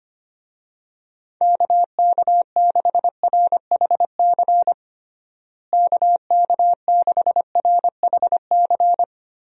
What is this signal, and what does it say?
Morse code, heard as a keyed tone, about 25 wpm, 700 Hz. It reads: KK6RHC KK6RHC
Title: USA Call Signs - 6 Characters - Repeated 3x - 25wpm